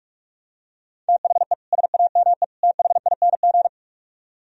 Morse code, heard as a keyed tone, about 46 wpm, 700 Hz. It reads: THE SAME THING